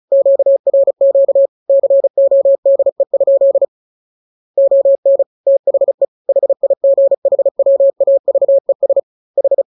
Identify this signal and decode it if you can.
Morse code, heard as a keyed tone, about 36 wpm, 550 Hz. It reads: QRQ CODE? ON THE HIGHWAVES H